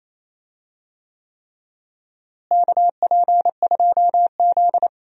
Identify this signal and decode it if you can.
Morse code, heard as a keyed tone, about 28 wpm, 700 Hz. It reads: KP2Z
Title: DXCC-100 Call Signs - 4 Characters - Repeated 3x - 28wpm